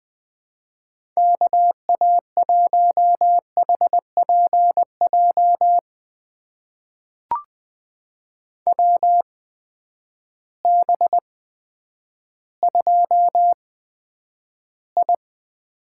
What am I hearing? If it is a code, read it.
Morse code, heard as a keyed tone, about 20 wpm, spaced as for 10 wpm, 700 Hz. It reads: KA1HPJ WB2I
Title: USA Call Signs - 6 Characters - 8x Character Spacing - 20wpm